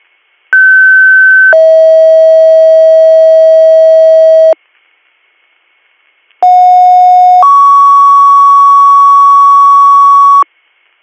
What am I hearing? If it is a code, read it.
Two-tone page, heard as a one-second tone, then a three-second tone, about 1550 then 640 Hz, about 730 then 1120 Hz. It is heard from 0.5 to 10.4 s.